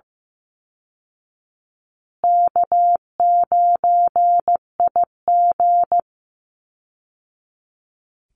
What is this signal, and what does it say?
Morse code, heard as a keyed tone, about 14 wpm, 700 Hz. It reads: K9IG
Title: Easy Call Signs Repeated 3x - 15wpm